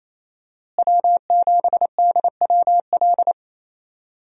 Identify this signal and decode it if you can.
Morse code, heard as a keyed tone, about 28 wpm, 700 Hz. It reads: W7DWL
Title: USA Call Signs - 5 Characters - Repeated 2x - 28wpm